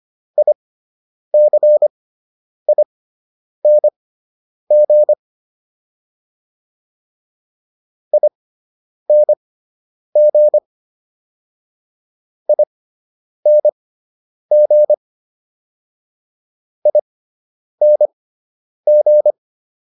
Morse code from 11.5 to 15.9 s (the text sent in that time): ING